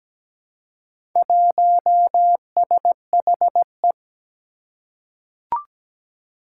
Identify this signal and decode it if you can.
Morse code, heard as a keyed tone, about 17 wpm, 700 Hz. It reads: 1SHE
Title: Four Challenging Letter-Number Combinations 17wpm